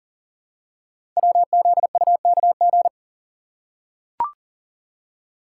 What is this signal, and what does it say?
Morse code, heard as a keyed tone, about 40 wpm, 690 Hz. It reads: WZUKG
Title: Code Groups - 5 Letters - 40wpm